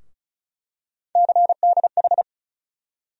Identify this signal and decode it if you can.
Morse code, heard as a keyed tone, about 36 wpm, 700 Hz. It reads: CDH
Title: USA Call Sign Postfixes - 3 Letters 35wpm